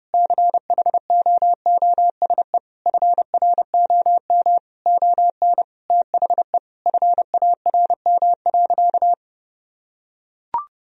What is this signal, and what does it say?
Morse code, heard as a keyed tone, about 30 wpm, 700 Hz. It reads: CHOOSE FROM ON THE FARM.
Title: Sentences from Top 1000 Words - Review Part 4 - 30wpm